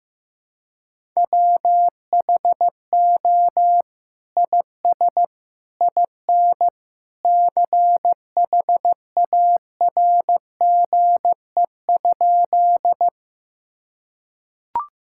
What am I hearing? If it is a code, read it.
Morse code, heard as a keyed tone, about 16 wpm, 700 Hz. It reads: WHO IS IN CHARGE?
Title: Sentences from Top 1000 Words - Review Part 6 - 15wpm